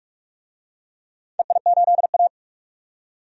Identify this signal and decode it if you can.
Morse code, heard as a keyed tone, about 46 wpm, 690 Hz. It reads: EI8A
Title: DXCC-100 Call Signs - 4 Characters - 45wpm